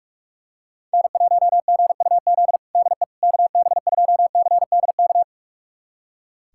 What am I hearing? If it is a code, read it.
Morse code, heard as a keyed tone, about 46 wpm, 700 Hz. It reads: N1GUZ DE KB2CDK
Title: USA Callsign DE Callsign 45wpm